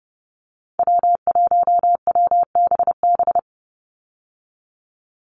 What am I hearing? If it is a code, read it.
Morse code, heard as a keyed tone, about 30 wpm, 700 Hz. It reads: W1WBB
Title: Call Signs Repeated 2x - 30wpm